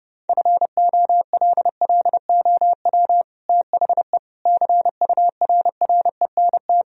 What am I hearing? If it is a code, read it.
Morse code, heard as a keyed tone, about 29 wpm, 700 Hz. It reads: FOLLOW THE CURRENT
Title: Sentences from Top 1000 Words - Review Part 4 - 30wpm